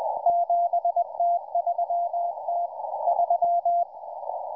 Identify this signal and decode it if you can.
Morse code, heard as a keyed tone, about 20 wpm, 690 Hz, with radio noise. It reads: 7T3T3